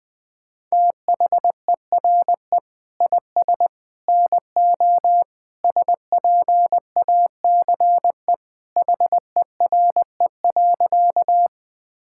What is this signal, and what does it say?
Morse code, heard as a keyed tone, about 21 wpm, 700 Hz. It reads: THERE IS NO SPACE HERE.
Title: Sentences from Top 1000 Words - Review Part 5 - 20wpm